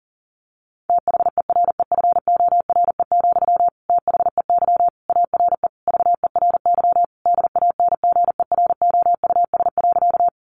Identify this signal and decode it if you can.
Morse code, heard as a keyed tone, about 40 wpm, 700 Hz. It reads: THEREFORE, THEY ARE VERY DANGEROUS.
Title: Sentences from Top 1200 Words 40wpm